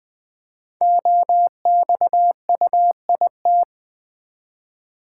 Morse code, heard as a keyed tone, about 20 wpm, 700 Hz. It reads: OXUIT
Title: Code Groups - 5 Letters - 20wpm